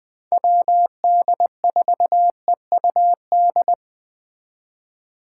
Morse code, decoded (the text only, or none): WD4EUD